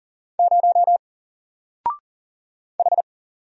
Morse code, heard as a keyed tone, about 38 wpm, 700 Hz. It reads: 0 H
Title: Single Letter-Number 40wpm